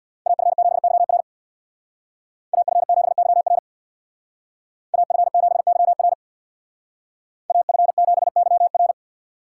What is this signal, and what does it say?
Morse code, heard as a keyed tone, about 65 wpm, 700 Hz. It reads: AF7YR AF7YR AF7YR AF7YR